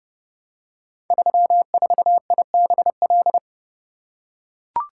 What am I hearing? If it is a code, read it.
Morse code, heard as a keyed tone, about 30 wpm, 700 Hz. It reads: 34IBL
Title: Code Groups - 5 Letter:Numbers - 30wpm